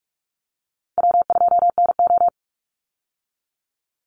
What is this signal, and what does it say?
Morse code, heard as a keyed tone, about 44 wpm, 700 Hz. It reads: W2NO
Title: Easy Call Signs Repeated 2x - 45wpm